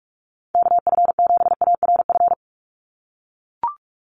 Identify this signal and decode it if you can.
Morse code, heard as a keyed tone, about 44 wpm, 700 Hz. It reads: KF7ARF
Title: USA Call Signs 45wpm